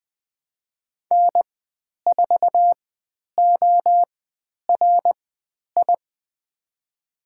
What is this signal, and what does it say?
Morse code, heard as a keyed tone, about 20 wpm, spaced as for 10 wpm, 700 Hz. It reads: N4ORI
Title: 095 Lesson 16g - Letter c - Call Signs 20wpm